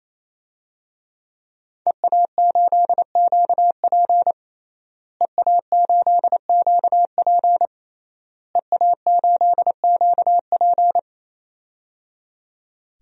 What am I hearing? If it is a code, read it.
Morse code, heard as a keyed tone, about 28 wpm, 700 Hz. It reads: EA8QP EA8QP EA8QP
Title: DXCC-100 Call Signs - 5 Characters - Repeated 3x - 28wpm